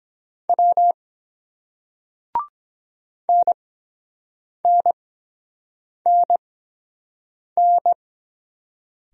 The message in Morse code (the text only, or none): W NNNN